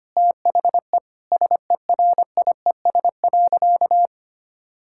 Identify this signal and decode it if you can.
Morse code, heard as a keyed tone, about 25 wpm, 700 Hz. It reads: THE SERIES.
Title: Sentences from Top 1000 Words - Review Part 5 - 25wpm